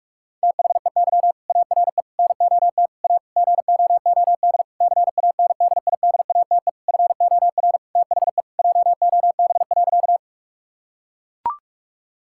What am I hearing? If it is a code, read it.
Morse code, heard as a keyed tone, about 43 wpm, 700 Hz. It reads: THEY ARE NOT A GOOD CANDIDATE FOR THE JOB.
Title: Sentences from Top 1600 Words 45wpm